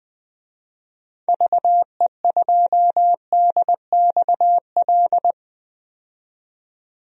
Morse code, heard as a keyed tone, about 21 wpm, 700 Hz. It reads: VE2DXL
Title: DXCC-100 Call Signs - 6 Characters - 20wpm